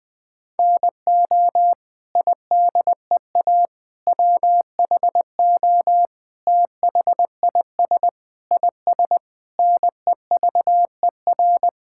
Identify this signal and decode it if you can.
Morse code, heard as a keyed tone, about 21 wpm, 700 Hz. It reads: NO IDEA WHO THIS IS NEVER
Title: Sets of 9 Words - 20wpm